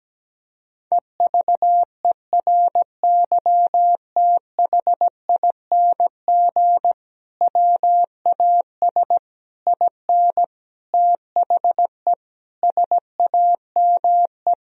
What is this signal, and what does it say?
Morse code, heard as a keyed tone, about 17 wpm, 700 Hz. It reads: EVERYTHING WAS IN THE SAME
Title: Sentences from Top 1000 Words - Review Part 2 - 17wpm